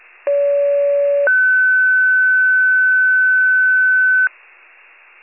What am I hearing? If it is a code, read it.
Two-tone page, heard as a one-second tone, then a three-second tone, about 570 then 1500 Hz. It is heard from 0.3 to 4.3 s.